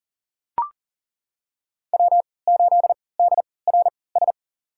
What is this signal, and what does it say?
Morse code, heard as a keyed tone, about 38 wpm, spaced as for 22 wpm, 700 Hz. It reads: W8DRS
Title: USA Call Signs - 5 Characters - 3x Character Spacing - 40wpm